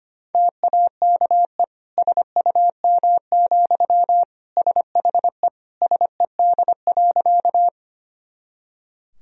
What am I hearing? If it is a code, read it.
Morse code, heard as a keyed tone, about 25 wpm, 700 Hz. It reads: TAKE SUM, SHE SED.
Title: Sentences from Top 200 Words and Abbreviations 25wpm